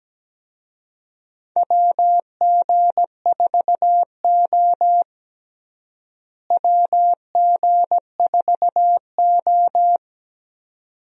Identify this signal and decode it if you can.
Morse code, heard as a keyed tone, about 17 wpm, 700 Hz. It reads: WG4O WG4O
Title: USA Call Signs - 4 Characters - Repeated 2x - 17wpm